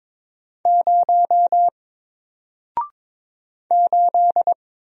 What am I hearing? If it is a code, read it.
Morse code, heard as a keyed tone, about 22 wpm, 700 Hz. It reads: 0 8
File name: Single-Letter-Number-22wpm